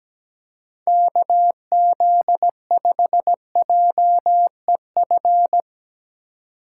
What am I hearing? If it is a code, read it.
Morse code, heard as a keyed tone, about 17 wpm, 700 Hz. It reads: KZ5JEF
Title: USA Call Signs - 6 Characters - 4x Character Spacing - 17wpm